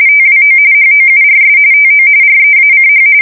RTTY, about 45.45 baud. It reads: TWO HOURS TO REACH
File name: fsk170 / USS-New-Jersey